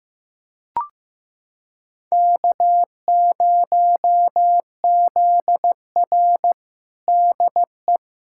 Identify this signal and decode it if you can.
Morse code, heard as a keyed tone, about 16 wpm, 700 Hz. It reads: K0ZR DE
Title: Practice QSO - Exchange 4 - Ending - 15wpm